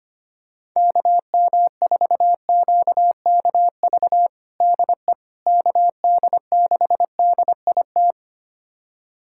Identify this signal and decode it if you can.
Morse code, heard as a keyed tone, about 25 wpm, 700 Hz. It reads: KM4QKV DE KD6DIT
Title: USA Callsign DE Callsign - 6 Characters - 25wpm